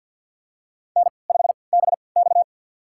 Morse code, heard as a keyed tone, about 50 wpm, spaced as for 28 wpm, 700 Hz. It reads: N5BX
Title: USA Call Signs - 4 Characters - 3x Character Spacing - 50wpm